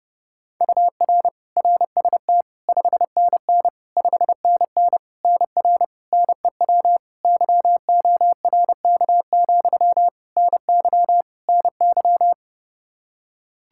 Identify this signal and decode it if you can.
Morse code, heard as a keyed tone, about 29 wpm, 700 Hz. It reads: UR RST 5NN 5NN NR NEW YORK, NY NY
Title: K1USN SST QSO Practice 30wpm